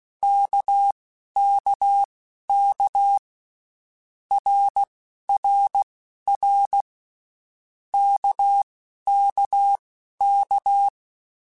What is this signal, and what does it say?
Morse code, heard as a keyed tone, about 16 wpm, spaced as for 12 wpm, 800 Hz. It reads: KKK RRR KKK